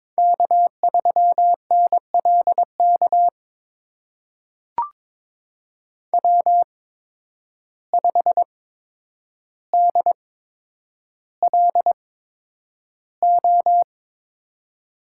Morse code, heard as a keyed tone, about 22 wpm, spaced as for 9 wpm, 700 Hz. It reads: K3NLK W5DLO
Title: USA Call Signs - 5 Characters - 8x Character Spacing - 22wpm